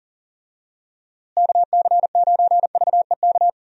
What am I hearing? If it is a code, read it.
Morse code, heard as a keyed tone, about 37 wpm, 700 Hz. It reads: KC9VEK
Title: USA Call Signs - 6 Characters - Repeated 3x - 40wpm